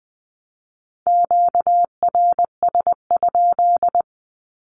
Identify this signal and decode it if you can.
Morse code, heard as a keyed tone, about 20 wpm, 700 Hz. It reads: QRS?
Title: Q-Codes 20wpm